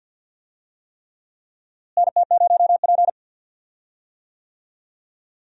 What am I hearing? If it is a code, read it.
Morse code, heard as a keyed tone, about 47 wpm, 700 Hz. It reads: NT0P
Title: USA Call Signs - 4 Characters - Repeated 3x - 50wpm